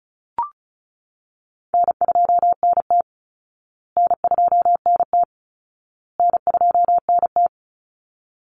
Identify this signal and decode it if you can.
Morse code, heard as a keyed tone, about 34 wpm, 700 Hz. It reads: N2NT N2NT N2NT
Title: Call Signs Repeated 2x - 35wpm